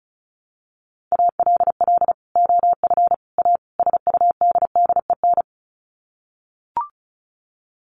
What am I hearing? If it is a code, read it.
Morse code, heard as a keyed tone, about 34 wpm, 700 Hz. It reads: ALL OF A SUDDEN
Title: Sets of 4 Words 35wpm